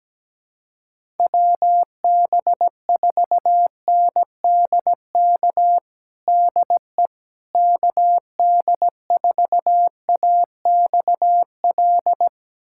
Morse code, heard as a keyed tone, about 17 wpm, 700 Hz. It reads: WB4NDK DE KD4AXL
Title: USA Callsign DE Callsign - 6 Characters - 17wpm